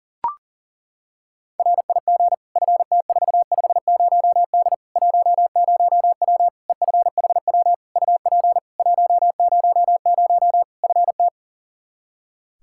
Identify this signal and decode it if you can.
Morse code, heard as a keyed tone, about 40 wpm, 700 Hz. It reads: RIG FT450D 10W EFHW UP 100 FT
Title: K1USN SST QSO Practice 40wpm